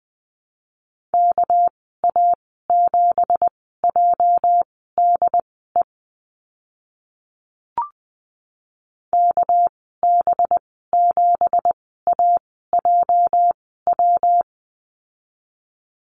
Morse code, heard as a keyed tone, about 20 wpm, spaced as for 14 wpm, 700 Hz. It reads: KA7JDE KB7AJW
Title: 231 Lesson 37g - Letter x - Call Signs 20wpm